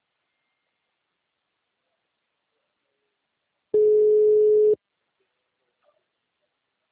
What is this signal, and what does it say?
Ringback tone (Europe, 425 Hz).